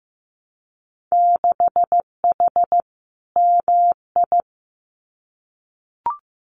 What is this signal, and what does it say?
Morse code, heard as a keyed tone, about 15 wpm, 700 Hz. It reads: 6H MI
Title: Winter Field Day Exchanges Encore 15wpm